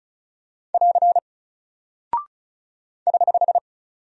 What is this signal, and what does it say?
Morse code, heard as a keyed tone, about 36 wpm, 700 Hz. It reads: <AR> <HH>
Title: Single Character (Letter:Number:Punctuation:Prosign) 35wpm